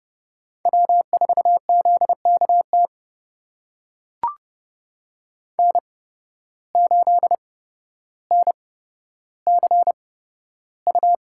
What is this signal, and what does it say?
Morse code, heard as a keyed tone, about 32 wpm, spaced as for 13 wpm, 700 Hz. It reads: W4ZKT N8NCU
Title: USA Call Signs - 5 Characters - 8x Character Spacing - 30wpm